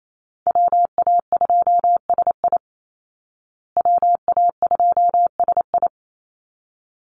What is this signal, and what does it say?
Morse code, heard as a keyed tone, about 28 wpm, 700 Hz. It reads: WA2SI WA2SI
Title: Call Signs Repeated 3x - 28wpm